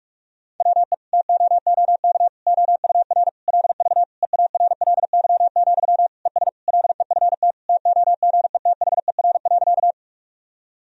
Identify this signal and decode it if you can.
Morse code, heard as a keyed tone, about 41 wpm, 700 Hz. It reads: WE TOOK OUR LV EARLY, ES LEFT TOGETHER.
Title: Sentences from Top 400 Words and Abbreviations 45wpm